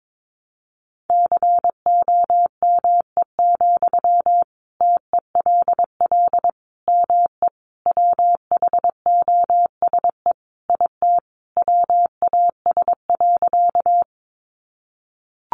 Morse code, heard as a keyed tone, about 22 wpm, 700 Hz. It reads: COME, TELL ME WHOSE IT WAS.
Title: Sentences from Top 1200 Words 22wpm